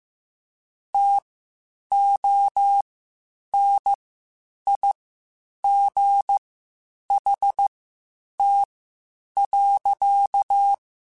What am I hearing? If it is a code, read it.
Morse code, heard as a keyed tone, about 15 wpm, spaced as for 8 wpm, 800 Hz. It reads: TONIGHT.